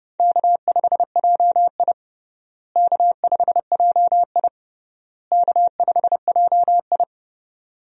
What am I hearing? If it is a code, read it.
Morse code, heard as a keyed tone, about 30 wpm, 700 Hz. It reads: K5JI K5JI K5JI
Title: USA Call Signs - 4 Characters - Repeated 3x - 30wpm